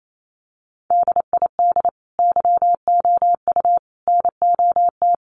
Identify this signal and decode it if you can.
Morse code, heard as a keyed tone, about 28 wpm, 700 Hz. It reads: DID YOU NOT
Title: Sentences from Top 600 Words 28wpm